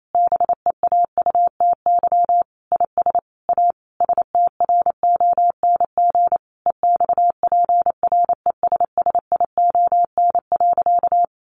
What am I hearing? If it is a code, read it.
Morse code, heard as a keyed tone, about 28 wpm, 700 Hz. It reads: BEAUTY IS A STRONG EXPRESSION.
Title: Sentences from Top 1200 Words 28wpm